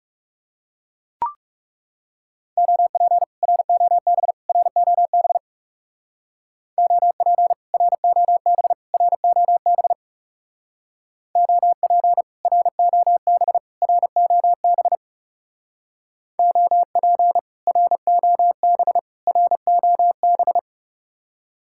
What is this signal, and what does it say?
Morse code, heard as a keyed tone, about 35 wpm, 700 Hz. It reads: OP ROB ROB OP ROB ROB OP ROB ROB OP ROB ROB
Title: Speed-Racing Common QSO Phrases - 30wpm